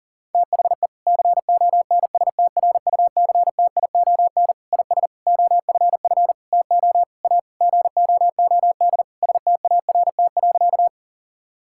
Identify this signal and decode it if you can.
Morse code, heard as a keyed tone, about 40 wpm, 700 Hz. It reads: THE CONSTRUCTION IS OFF TO A GOOD START.
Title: Sentences from Top 1600 Words 40wpm